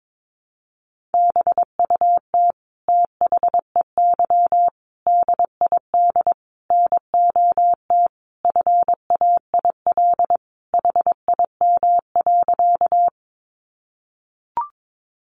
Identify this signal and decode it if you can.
Morse code, heard as a keyed tone, about 22 wpm, 700 Hz. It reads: BUT THEY DID NOT FAIL HIM.
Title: Sentences from Top 1100 Words 22wpm